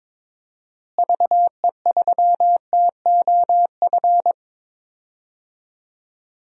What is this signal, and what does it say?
Morse code, heard as a keyed tone, about 22 wpm, 700 Hz. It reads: VE3TOF
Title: DXCC-100 Call Signs - 6 Characters - Repeated 3x - 22wpm